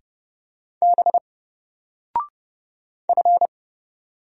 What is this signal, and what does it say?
Morse code, heard as a keyed tone, about 30 wpm, 700 Hz. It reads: B F
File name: Single-Letter-Number-30wpm